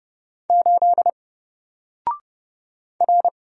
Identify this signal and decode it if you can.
Morse code, heard as a keyed tone, about 30 wpm, 700 Hz. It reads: 8 R
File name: Single-Letter-Number-30wpm